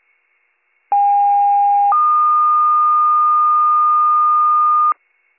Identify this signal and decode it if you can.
Two-tone page, heard as a one-second tone, then a three-second tone, about 810 then 1220 Hz. It is heard from 0.9 to 4.9 s.